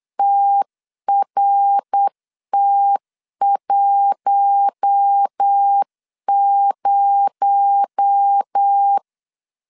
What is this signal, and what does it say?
Morse code, heard as a keyed tone, about 8 wpm, 800 Hz. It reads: TRT10
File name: CW / DET / 1 9 1500C S A 8wpm 5637